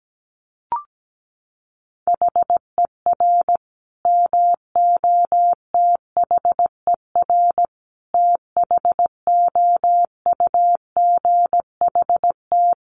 Morse code, heard as a keyed tone, about 17 wpm, 700 Hz. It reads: HER MOTHER THOUGHT
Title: Sentences from Top 800 Words 17wpm